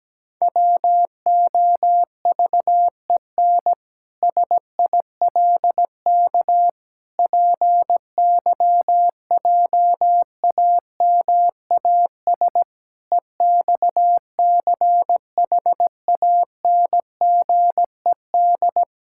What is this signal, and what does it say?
Morse code, heard as a keyed tone, about 17 wpm, 700 Hz. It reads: WOVEN SILK PYJAMAS EXCHANGED
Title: Pangram Challenge v2 - 17wpm